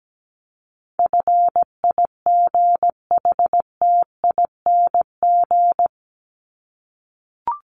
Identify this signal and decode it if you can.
Morse code, heard as a keyed tone, about 17 wpm, 700 Hz. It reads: FIGHTING
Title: ING Suffix - Top 100 Words with Base Words 17wpm